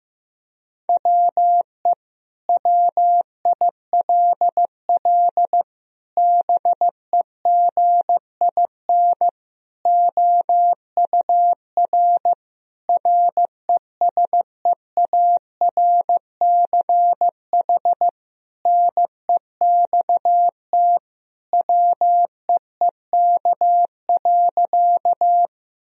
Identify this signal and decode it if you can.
Morse code, heard as a keyed tone, about 15 wpm, 700 Hz. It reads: WE WILL BEGIN OUR RESEARCH NEXT WEEK.
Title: Sentences from Top 1000 Words - Review Part 4 - 15wpm